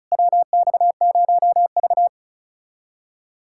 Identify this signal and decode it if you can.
Morse code, heard as a keyed tone, about 34 wpm, 700 Hz. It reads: WX0V
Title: K1USN SST QSO Practice - 35wpm